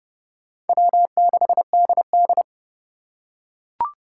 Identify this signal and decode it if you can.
Morse code, heard as a keyed tone, about 29 wpm, 700 Hz. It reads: W6DD